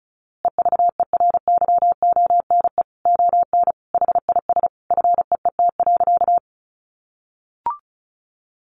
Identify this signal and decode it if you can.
Morse code, heard as a keyed tone, about 34 wpm, 700 Hz. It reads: EVERYONE ON HIS FEET.
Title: A Sound Of Thunder 35wpm